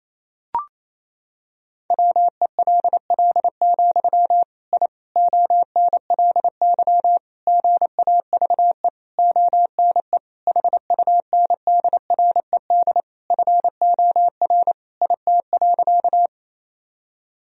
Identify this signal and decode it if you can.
Morse code, heard as a keyed tone, about 28 wpm, 700 Hz. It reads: WELL, I ONLY GAVE ONE HUNDRED FOR IT.
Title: Sentences from Top 1000 Words - Review Part 2 - 28wpm